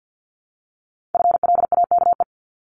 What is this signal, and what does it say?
Morse code, heard as a keyed tone, about 50 wpm, 700 Hz. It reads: FLAKE
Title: Multi-letter Phonemes 50wpm